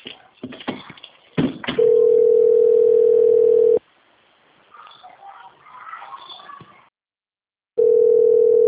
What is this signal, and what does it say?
Ringback tone (North America), 440 + 480 Hz.